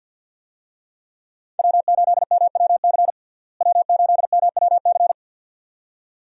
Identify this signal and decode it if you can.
Morse code, heard as a keyed tone, about 49 wpm, 700 Hz. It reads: W8MWC W8MWC
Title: USA Call Signs - 5 Characters - Repeated 2x - 50wpm